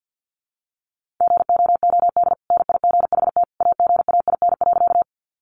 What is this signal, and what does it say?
Morse code, heard as a keyed tone, about 50 wpm, 700 Hz. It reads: GOOD NIGHT AGAIN.
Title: Sentences from Top 300 Words 50wpm